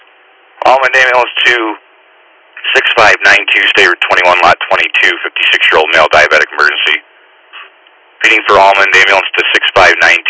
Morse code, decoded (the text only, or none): none